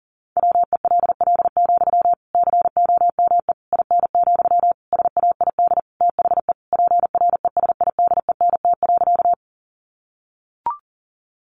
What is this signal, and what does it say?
Morse code, heard as a keyed tone, about 40 wpm, 700 Hz. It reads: WELL, COME IN, SAID THE PRESIDENT.
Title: Sentences from Top 1000 Words 40wpm